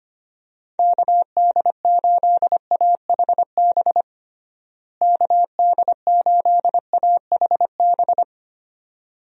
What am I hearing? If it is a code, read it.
Morse code, heard as a keyed tone, about 25 wpm, 700 Hz. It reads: KD8AHB KD8AHB
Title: USA Call Signs - 6 Characters - Repeated 2x - 25wpm